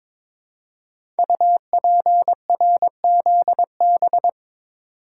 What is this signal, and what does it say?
Morse code, heard as a keyed tone, about 23 wpm, 700 Hz. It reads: UPRZB
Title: Code Groups - 5 Letters - 22wpm